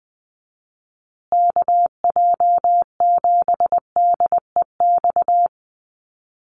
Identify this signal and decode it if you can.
Morse code, heard as a keyed tone, about 20 wpm, 700 Hz. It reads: KJ7DEX
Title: USA Call Signs Repeated 3x 20wpm